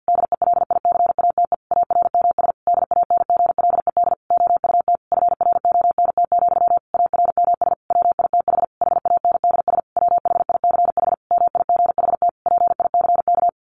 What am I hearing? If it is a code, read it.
Morse code, heard as a keyed tone, about 50 wpm, 700 Hz. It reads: DELICATE ARMS DANGLED OUT FRONT, ARMS WITH HANDS WHICH MIGHT PICK
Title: A Sound Of Thunder 50wpm